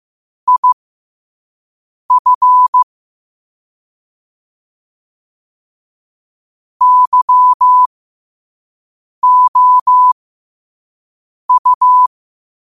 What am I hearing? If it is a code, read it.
Morse code, heard as a keyed tone, about 15 wpm, spaced as for 5 wpm, 1000 Hz. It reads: IF YOU
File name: qotdfiles / 5wpm